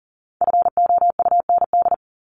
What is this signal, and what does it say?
Morse code, heard as a keyed tone, about 40 wpm, 700 Hz. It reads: FOUND
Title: Top 500 Words 40wpm